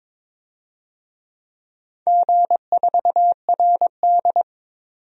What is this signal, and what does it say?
Morse code, heard as a keyed tone, about 23 wpm, 700 Hz. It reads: G4RD
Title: DXCC-100 Call Signs - 4 Characters - 22wpm